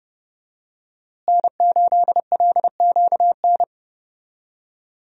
Morse code, heard as a keyed tone, about 30 wpm, 700 Hz. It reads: N8LQN
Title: USA Call Signs - 5 Characters - Repeated 2x - 30wpm